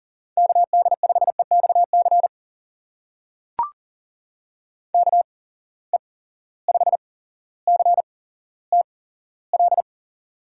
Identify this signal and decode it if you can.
Morse code, heard as a keyed tone, about 40 wpm, spaced as for 17 wpm, 700 Hz. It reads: KD5EXC KE5CTL
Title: USA Call Signs - 6 Characters - 8x Character Spacing - 40wpm